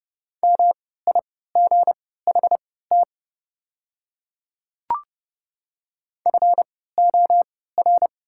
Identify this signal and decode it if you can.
Morse code, heard as a keyed tone, about 30 wpm, spaced as for 17 wpm, 700 Hz. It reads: MIGHT FOR
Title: 3x Character Spacing - Top 300 Words - 30wpm